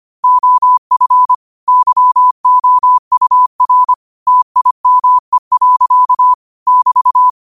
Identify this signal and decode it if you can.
Morse code, heard as a keyed tone, about 25 wpm, 1000 Hz. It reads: OF YOUR TIME. <BT>